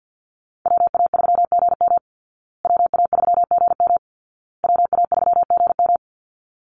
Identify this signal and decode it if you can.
Morse code, heard as a keyed tone, about 50 wpm, 700 Hz. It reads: WA3GM WA3GM WA3GM
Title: Call Signs Repeated 2x - 50wpm